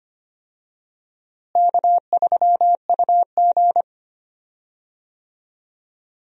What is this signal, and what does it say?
Morse code, heard as a keyed tone, about 25 wpm, 700 Hz. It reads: K3UG
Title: USA Call Signs - 4 Characters - Repeated 3x - 25wpm